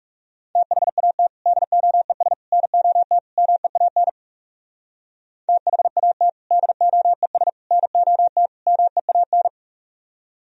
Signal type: Morse code

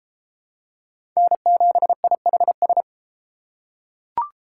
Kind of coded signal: Morse code

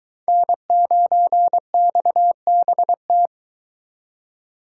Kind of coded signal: Morse code

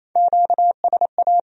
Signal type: Morse code

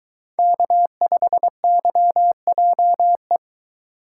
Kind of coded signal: Morse code

